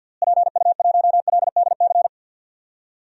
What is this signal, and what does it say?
Morse code, heard as a keyed tone, about 51 wpm, 700 Hz. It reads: PU1LDC
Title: DXCC-100 Call Signs - 6 Characters - Repeated 2x - 50wpm